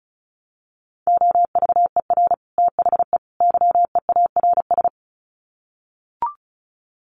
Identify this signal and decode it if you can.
Morse code, heard as a keyed tone, about 34 wpm, 700 Hz. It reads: OVER THE YEARS
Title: Sets of 3 Words 35wpm